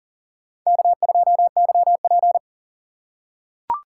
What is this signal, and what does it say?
Morse code, heard as a keyed tone, about 40 wpm, 700 Hz. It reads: K2YP
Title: USA Call Signs - 4 Characters - 4x Character Spacing - 40wpm